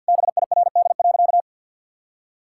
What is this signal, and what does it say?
Morse code, heard as a keyed tone, about 52 wpm, 700 Hz. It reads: BIRD.
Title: Sentences from Top 1000 Words - Review Part 6 - 50wpm